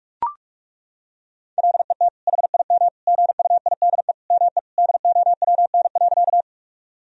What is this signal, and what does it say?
Morse code, heard as a keyed tone, about 41 wpm, 700 Hz. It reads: LET HIM GUIDE ME DOWN.
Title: Sentences from Top 1000 Words - Review Part 3 - 45wpm